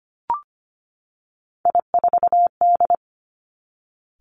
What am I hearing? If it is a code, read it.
Morse code, heard as a keyed tone, about 25 wpm, 700 Hz. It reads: I4D
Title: Three Challenging Letter-Number Combinations 25wpm